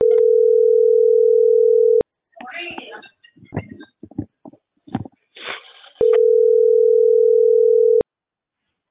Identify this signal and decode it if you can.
Ringback tone (North America), 440 + 480 Hz.